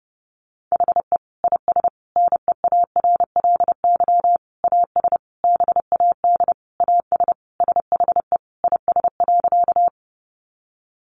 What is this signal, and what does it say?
Morse code, heard as a keyed tone, about 30 wpm, 700 Hz. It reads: HE IS NEARLY AS BAD AS SHE IS.
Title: Sentences from Top 1300 Words 30wpm